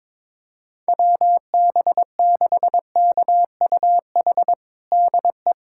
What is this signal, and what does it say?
Morse code, heard as a keyed tone, about 22 wpm, 700 Hz. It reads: WB6KUH DE
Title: USA Callsign DE Callsign - 6 Characters - 22wpm